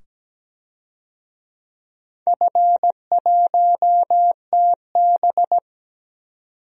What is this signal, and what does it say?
Morse code, heard as a keyed tone, about 17 wpm, 700 Hz. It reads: F1TB
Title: DXCC-100 Call Signs - 4 Characters - 17wpm